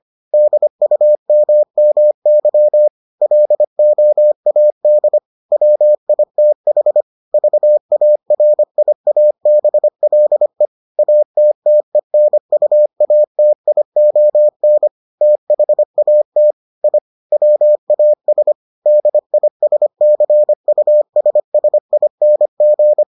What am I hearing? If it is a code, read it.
Morse code, heard as a keyed tone, about 25 wpm, 600 Hz. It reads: DUMMY LOAD WITH VARIABLE ATTENUATION THAT I WAS DISCUSSING